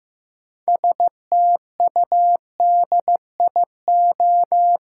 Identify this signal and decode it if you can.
Morse code, heard as a keyed tone, about 15 wpm, 700 Hz. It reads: STUDIO
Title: Sentences from Top 1200 Words - New Word Drill - 15wpm